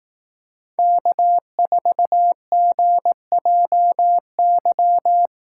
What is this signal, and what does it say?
Morse code, heard as a keyed tone, about 18 wpm, 700 Hz. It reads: K4GJY
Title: Speed Racing - USA 1x3 Call Signs 15wpm